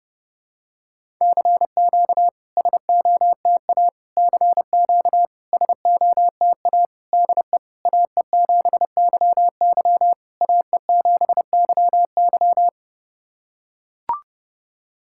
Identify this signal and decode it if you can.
Morse code, heard as a keyed tone, about 30 wpm, 700 Hz. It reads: CQ SOTA CQ SOTA DE AE7YY AE7YY
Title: SOTA QSO Practice 30wpm